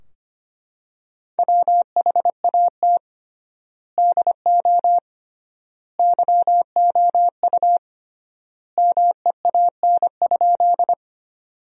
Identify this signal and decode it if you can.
Morse code, heard as a keyed tone, about 25 wpm, 700 Hz. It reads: WHAT DO YOU MEAN?
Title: 3x Word-Spacing - Sentences from Top 200 Words - 25wpm